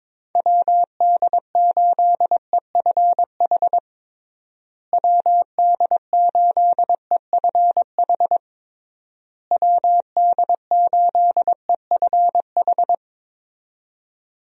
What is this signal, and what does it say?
Morse code, heard as a keyed tone, about 22 wpm, 700 Hz. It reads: WD8EFH WD8EFH WD8EFH
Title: USA Call Signs - 6 Characters - Repeated 3x - 22wpm